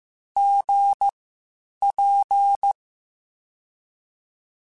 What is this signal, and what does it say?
Morse code, heard as a keyed tone, about 15 wpm, spaced as for 8 wpm, 800 Hz. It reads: GP
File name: QSOs / QSOs-mp3s / WD0BGP 1@10